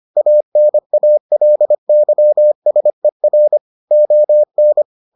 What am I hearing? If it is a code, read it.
Morse code, heard as a keyed tone, about 25 wpm, 600 Hz. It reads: ANALYSER ON